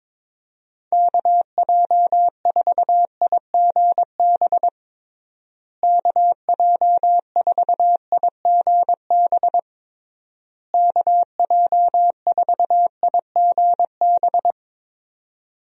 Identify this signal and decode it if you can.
Morse code, heard as a keyed tone, about 22 wpm, 700 Hz. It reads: KJ4IGB KJ4IGB KJ4IGB
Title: USA Call Signs - 6 Characters - Repeated 3x - 22wpm